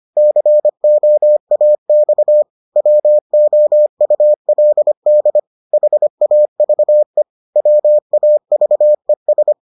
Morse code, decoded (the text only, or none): COAX WOULD HAVE WAVES